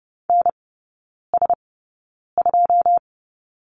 Morse code, heard as a keyed tone, about 30 wpm, spaced as for 9 wpm, 700 Hz. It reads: NS2
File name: 081 Lesson 14e - Number 2 - Three Characters 30wpm